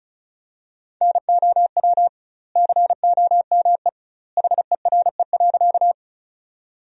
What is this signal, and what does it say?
Morse code, heard as a keyed tone, about 36 wpm, 700 Hz. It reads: NOW COME HERE.
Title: 2x Word-Spacing - Sentences from Top 200 Words - 35wpm